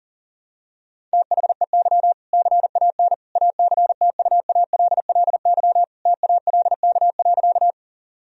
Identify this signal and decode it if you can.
Morse code, heard as a keyed tone, about 40 wpm, 700 Hz. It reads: THEY CAN ACTUALLY TALK.
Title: Sentences from Top 1000 Words - Review Part 4 - 40wpm